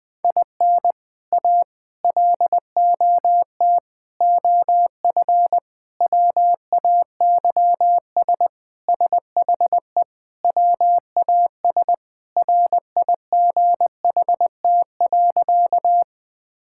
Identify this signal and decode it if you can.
Morse code, heard as a keyed tone, about 20 wpm, 700 Hz. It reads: IN A LOT OF WAYS SHE WAS RIGHT.
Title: Sentences with 4-Word Phrases - Spoken Beginning - 20wpm